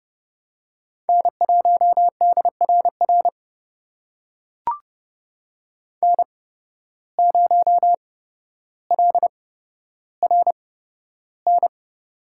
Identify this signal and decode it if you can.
Morse code, heard as a keyed tone, about 30 wpm, spaced as for 13 wpm, 700 Hz. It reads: N1DRR N0LRN
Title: USA Call Signs - 5 Characters - 8x Character Spacing - 30wpm